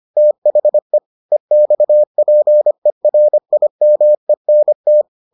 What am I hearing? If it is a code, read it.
Morse code, heard as a keyed tone, about 25 wpm, 600 Hz. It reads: THE EXPERIMENT